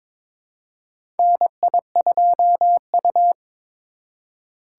Morse code, heard as a keyed tone, about 22 wpm, 700 Hz. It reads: NI2U
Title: USA Call Signs - 4 Characters - 3x Character Spacing - 22wpm